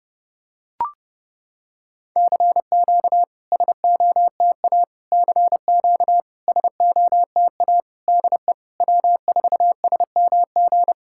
Morse code, heard as a keyed tone, about 30 wpm, 700 Hz. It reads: CQ SOTA CQ SOTA DE W4SMG
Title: SOTA QSO Practice 30wpm